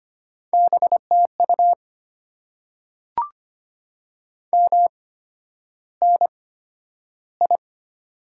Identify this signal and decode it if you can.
Morse code, heard as a keyed tone, about 25 wpm, spaced as for 11 wpm, 700 Hz. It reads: BTU MNI
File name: Top 90 QSO Elements - 8x Character Spacing - 25wpm